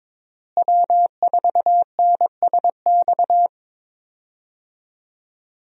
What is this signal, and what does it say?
Morse code, heard as a keyed tone, about 22 wpm, 700 Hz. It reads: W4NSX
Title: USA Call Signs - 5 Characters - Repeated 3x - 22wpm